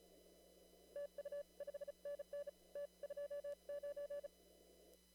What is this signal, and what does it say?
Morse code, heard as a keyed tone, about 34 wpm, spaced as for 30 wpm, 570 Hz. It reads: TU5NN T29
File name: CW / kt5j wpx cw-op-k5pi